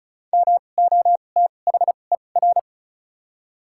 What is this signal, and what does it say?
Morse code, heard as a keyed tone, about 34 wpm, spaced as for 25 wpm, 700 Hz. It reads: MOTHER
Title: 2x Character Spacing - Top 200 Words - 35wpm